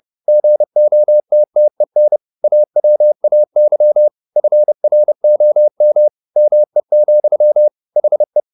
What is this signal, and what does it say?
Morse code, heard as a keyed tone, about 30 wpm, 600 Hz. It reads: GOTTEN AWAY FROM ME, HE
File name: the old man and the sea / the old man and the sea 1 30wpm